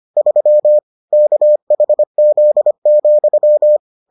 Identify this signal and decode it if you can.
Morse code, heard as a keyed tone, about 25 wpm, 600 Hz. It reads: 3 KHZ,